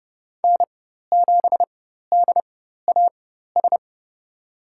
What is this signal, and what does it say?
Morse code, heard as a keyed tone, about 32 wpm, spaced as for 14 wpm, 700 Hz. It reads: N7DAS
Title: USA Call Signs - 5 Characters - 4x Character Spacing - 30wpm